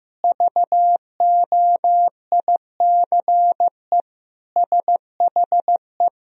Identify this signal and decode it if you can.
Morse code, heard as a keyed tone, about 15 wpm, 700 Hz. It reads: VOICE SHE
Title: Sentences from Top 500 Words and Abbreviations 15wpm